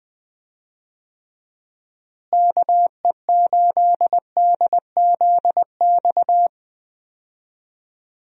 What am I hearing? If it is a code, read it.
Morse code, heard as a keyed tone, about 20 wpm, 700 Hz. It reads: KE8DZX
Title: USA Call Signs - 6 Characters - Repeated 3x - 20wpm